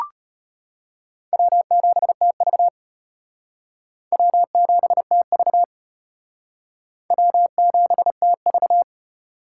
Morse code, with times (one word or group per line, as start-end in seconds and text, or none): 1.3-2.7 W7TV
4.1-5.6 W7TV
7.1-8.8 W7TV